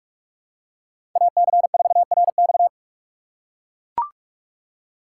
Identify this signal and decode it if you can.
Morse code, heard as a keyed tone, about 42 wpm, 700 Hz. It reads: AC4RX
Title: USA Call Signs - 3x Character Spacing - 45wpm